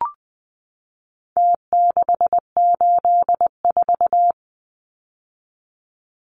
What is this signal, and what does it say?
Morse code, heard as a keyed tone, about 20 wpm, 700 Hz. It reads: T684
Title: Numbers - Contest Style - 4-Digit 20wpm